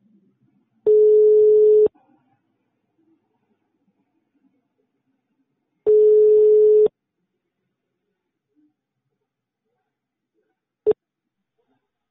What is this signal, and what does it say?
Ringback tone (Europe, 425 Hz).